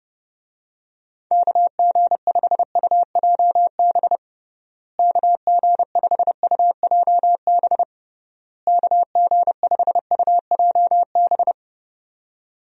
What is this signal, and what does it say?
Morse code, heard as a keyed tone, about 30 wpm, 700 Hz. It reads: KG5UJB KG5UJB KG5UJB
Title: USA Call Signs - 6 Characters - Repeated 3x - 30wpm